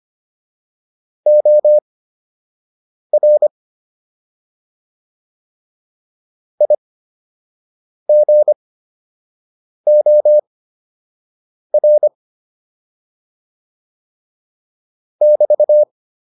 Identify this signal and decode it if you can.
Morse code, heard as a keyed tone, about 25 wpm, spaced as for 6 wpm, 600 Hz. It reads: OR IGOR <BT>
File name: QSO 204 06